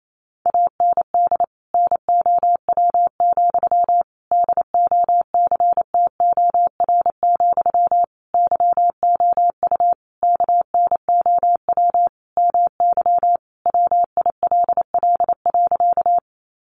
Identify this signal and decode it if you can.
Morse code, heard as a keyed tone, about 28 wpm, 700 Hz. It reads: AND NOW, DOCTOR, YOU KNOW MY WILL.
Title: Sentences from Top 1500 Words 28wpm